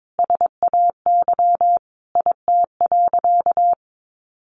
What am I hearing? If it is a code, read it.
Morse code, heard as a keyed tone, about 22 wpm, 700 Hz. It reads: SAY IT.
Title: Sentences from Top 400 Words 22wpm